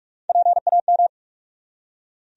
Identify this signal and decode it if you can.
Morse code, heard as a keyed tone, about 47 wpm, 700 Hz. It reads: PAM
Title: USA Common Names - Max 3 Letters - 45wpm